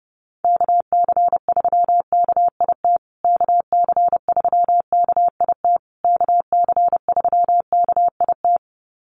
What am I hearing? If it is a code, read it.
Morse code, heard as a keyed tone, about 30 wpm, 700 Hz. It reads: KC3KIT KC3KIT KC3KIT
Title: USA Call Signs Repeated 3x 30wpm